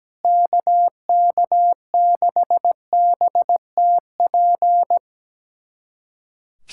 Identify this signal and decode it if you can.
Morse code, heard as a keyed tone, about 17 wpm, 700 Hz. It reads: KK6BTP
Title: USA Callsign DE Callsign 17wpm